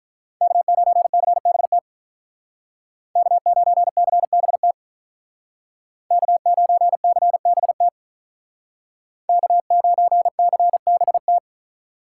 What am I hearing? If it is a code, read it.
Morse code, heard as a keyed tone, about 42 wpm, 700 Hz. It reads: K9CBT K9CBT K9CBT K9CBT
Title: Speed Racing - USA 1x3 Call Signs 35wpm